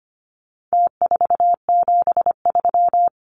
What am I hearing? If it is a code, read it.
Morse code, heard as a keyed tone, about 25 wpm, 700 Hz. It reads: T473
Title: Numbers - Contest Style - 4-Digit 25wpm